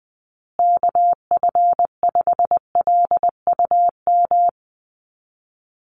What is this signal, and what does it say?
Morse code, heard as a keyed tone, about 20 wpm, 700 Hz. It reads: KF5LUM
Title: USA Call Signs Repeated 2x 20wpm